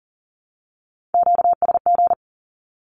Morse code, heard as a keyed tone, about 40 wpm, 700 Hz. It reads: QSG
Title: Q-Codes 40wpm